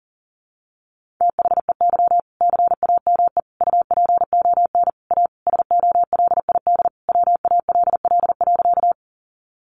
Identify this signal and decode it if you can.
Morse code, heard as a keyed tone, about 40 wpm, 700 Hz. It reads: THEY CAME UPON A SOLID WALL.
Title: Sentences from Top 1500 Words 40wpm